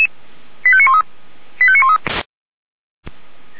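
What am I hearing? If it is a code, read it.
Five-tone selective call (ZVEI), heard as quick runs of five tones, tones of about 70 ms. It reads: 85913 85913
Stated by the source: ZVEI five-tone call 89135 received by SDR, from the Feuerwehr Orlamünde fire brigade audio archive.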